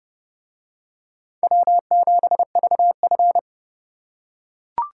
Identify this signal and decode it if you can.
Morse code, heard as a keyed tone, about 29 wpm, 700 Hz. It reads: W7VF